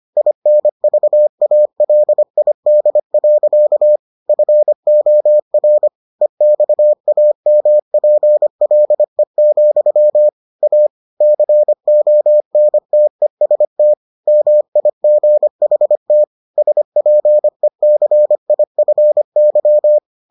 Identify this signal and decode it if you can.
Morse code, heard as a keyed tone, about 25 wpm, 600 Hz. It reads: INVALID. FOR EXAMPLE, A CONTEST MIGHT SPECIFY